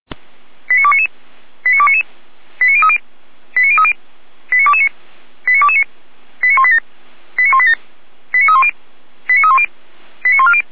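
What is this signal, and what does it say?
Five-tone selective call (ZVEI), heard as quick runs of five tones, tones of about 70 ms. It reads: 89299 89299 89030 89030 89229 89229 89187 89187 89310 89310 89140
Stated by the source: Five-tone ZVEI selective call 89174 received by SDR.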